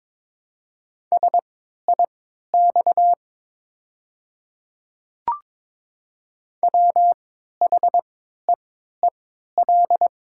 Morse code, heard as a keyed tone, about 22 wpm, spaced as for 12 wpm, 700 Hz. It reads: SIX WHEEL
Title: 3x Character Spacing - Top 500 Words - 22wpm